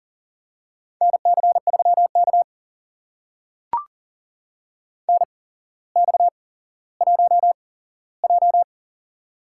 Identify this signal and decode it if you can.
Morse code, heard as a keyed tone, about 40 wpm, spaced as for 17 wpm, 700 Hz. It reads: NC3K NX1J
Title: USA Call Signs - 4 Characters - 8x Character Spacing - 40wpm